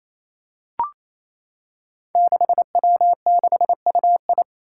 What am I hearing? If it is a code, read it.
Morse code, heard as a keyed tone, about 28 wpm, 700 Hz. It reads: 6W6UI
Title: Five Challenging Letter-Number Combinations 28wpm